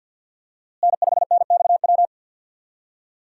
Morse code, heard as a keyed tone, about 49 wpm, 700 Hz. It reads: N5NXW
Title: USA Call Signs - 5 Characters - Repeated 2x - 50wpm